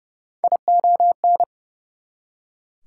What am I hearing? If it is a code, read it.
Morse code, heard as a keyed tone, about 29 wpm, 700 Hz. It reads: ION
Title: Prefixes and Suffixes 30wpm